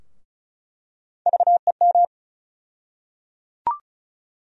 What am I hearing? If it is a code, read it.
Morse code, heard as a keyed tone, about 36 wpm, 700 Hz. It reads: VEM
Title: USA Call Sign Postfixes - 3 Letters 35wpm